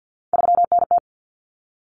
Morse code, heard as a keyed tone, about 50 wpm, 700 Hz. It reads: 3NT
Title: Common Cut Numbers - 3-Digit 50wpm